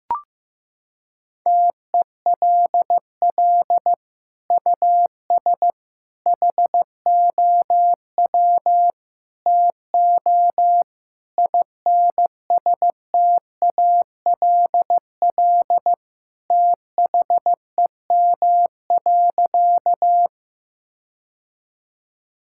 Morse code, heard as a keyed tone, about 15 wpm, 700 Hz. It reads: TELL US HOW TO INSTALL THEM.
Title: Sentences from Top 1600 Words 15wpm